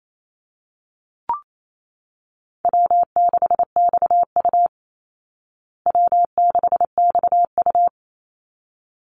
Morse code, heard as a keyed tone, about 28 wpm, 700 Hz. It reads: W6XU W6XU
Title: Easy Call Signs Repeated 2x - 28wpm